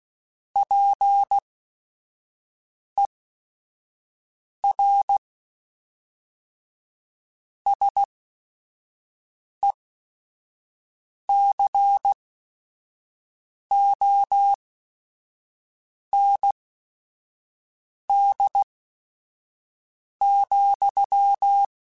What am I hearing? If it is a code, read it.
Morse code, heard as a keyed tone, about 16 wpm, spaced as for 5 wpm, 800 Hz. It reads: PER SECOND,
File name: Technical Subjects / Technical Subjects-mp3s / Reactance @5